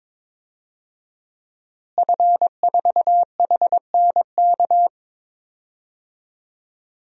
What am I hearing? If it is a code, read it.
Morse code, heard as a keyed tone, about 22 wpm, 700 Hz. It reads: F4HNK
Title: DXCC-100 Call Signs - 5 Characters - Repeated 2x - 22wpm